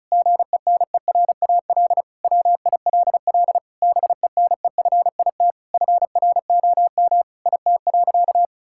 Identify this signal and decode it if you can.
Morse code, heard as a keyed tone, about 36 wpm, 700 Hz. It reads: GENERAL WILL BENEFIT FROM IT.
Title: Sentences from Top 1000 Words - Review Part 6 - 35wpm